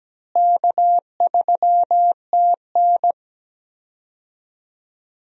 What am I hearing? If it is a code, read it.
Morse code, heard as a keyed tone, about 17 wpm, 700 Hz. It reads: K3TN
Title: USA Call Signs - 4 Characters - Repeated 3x - 17wpm